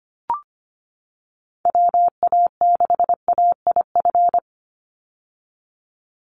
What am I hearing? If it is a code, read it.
Morse code, heard as a keyed tone, about 25 wpm, 700 Hz. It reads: WA6AIF
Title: USA Call Signs 25wpm